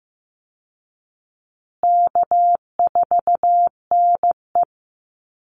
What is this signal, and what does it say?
Morse code, heard as a keyed tone, about 15 wpm, 700 Hz. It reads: K4NE
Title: Easy Call Signs Repeated 3x - 15wpm